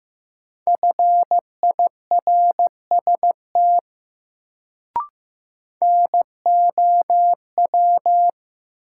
Morse code, heard as a keyed tone, about 15 wpm, 700 Hz. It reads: FIRST NOW
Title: Sentences from Top 100 Words - New Word Drill - 15wpm